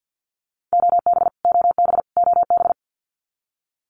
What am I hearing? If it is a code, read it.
Morse code, heard as a keyed tone, about 50 wpm, 700 Hz. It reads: OB OB OB
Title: Abbreviations Repeated 3x - 50wpm